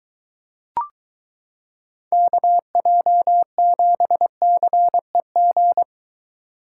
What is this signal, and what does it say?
Morse code, heard as a keyed tone, about 22 wpm, 700 Hz. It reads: KJ7CEG